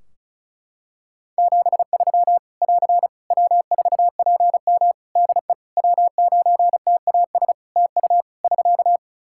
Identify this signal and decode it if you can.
Morse code, heard as a keyed tone, about 36 wpm, 700 Hz. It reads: 73 <AR> W4PM DE W9TAS TU <SK>
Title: K1USN SST QSO Practice 35wpm